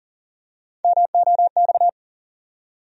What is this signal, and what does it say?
Morse code, heard as a keyed tone, about 40 wpm, 700 Hz. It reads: MOX
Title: USA Call Sign Postfixes - 3 Letters 40wpm